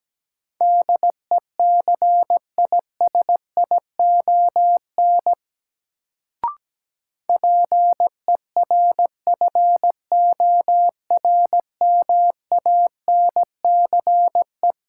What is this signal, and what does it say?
Morse code, heard as a keyed tone, about 17 wpm, 700 Hz. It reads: DECISION PERFORMANCE
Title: Sentences from Top 1000 Words - New Word Drill - 17wpm